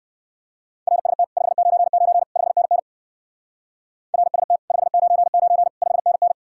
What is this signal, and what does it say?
Morse code, heard as a keyed tone, about 57 wpm, 700 Hz. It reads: RST 599 5NN RST 599 5NN